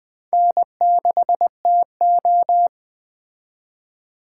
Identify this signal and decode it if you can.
Morse code, heard as a keyed tone, about 20 wpm, 700 Hz. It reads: N6TO